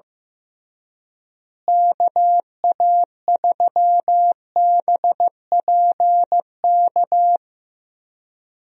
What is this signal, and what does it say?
Morse code, heard as a keyed tone, about 15 wpm, 700 Hz. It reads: KA3BPK
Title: USA Call Signs - 6 Characters - Repeated 3x - 15wpm